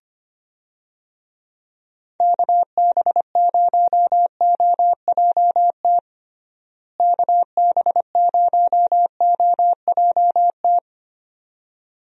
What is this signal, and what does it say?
Morse code, heard as a keyed tone, about 25 wpm, 700 Hz. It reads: KB0OJT KB0OJT
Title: USA Call Signs - 6 Characters - Repeated 2x - 25wpm